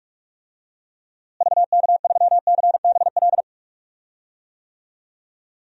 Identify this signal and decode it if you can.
Morse code, heard as a keyed tone, about 41 wpm, 700 Hz. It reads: VK3CBL
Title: DXCC-100 Call Signs - 6 Characters - Repeated 2x - 45wpm